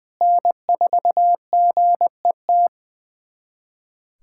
Morse code, heard as a keyed tone, about 20 wpm, 700 Hz. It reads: N4GET
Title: USA Callsign DE Callsign - 5 Characters - 20wpm